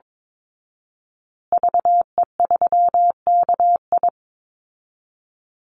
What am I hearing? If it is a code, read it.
Morse code, heard as a keyed tone, about 22 wpm, 700 Hz. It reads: VE3KI
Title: Call Signs Repeated 2x - 22wpm